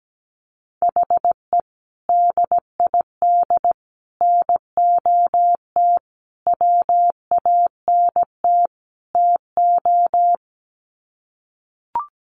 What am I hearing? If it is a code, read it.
Morse code, heard as a keyed tone, about 17 wpm, 700 Hz. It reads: HE DID NOT WANT TO
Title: Sets of 5 Words Encore 17wpm